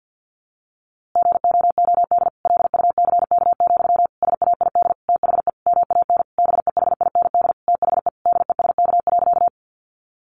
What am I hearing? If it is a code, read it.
Morse code, heard as a keyed tone, about 50 wpm, 700 Hz. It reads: GOOD LUCK, SAID THE MAN BEHIND THE DESK.
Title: A Sound Of Thunder 50wpm